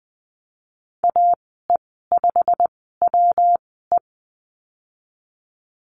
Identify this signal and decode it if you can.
Morse code, heard as a keyed tone, about 20 wpm, spaced as for 15 wpm, 700 Hz. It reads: AE5WE
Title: 173 Lesson 28g - Number 9 - Call Signs 20wpm